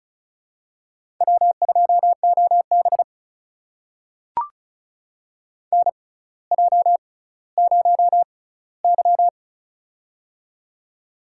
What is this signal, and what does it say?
Morse code, heard as a keyed tone, about 34 wpm, spaced as for 18 wpm, 700 Hz. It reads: W2OB NJ0Y
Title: USA Call Signs - 6x Character Spacing - 35wpm